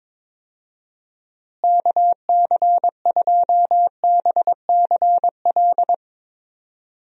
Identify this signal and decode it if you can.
Morse code, heard as a keyed tone, about 21 wpm, 700 Hz. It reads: KC2BCL
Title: USA Call Signs - 6 Characters - Repeated 2x - 22wpm